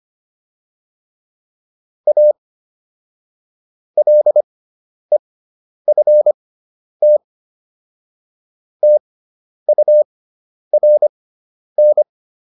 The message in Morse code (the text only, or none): A LEFT TURN